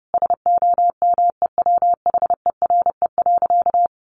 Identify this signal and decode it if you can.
Morse code, heard as a keyed tone, about 30 wpm, 700 Hz. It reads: SOMEWHERE.
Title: Sentences from Top 1400 Words 30wpm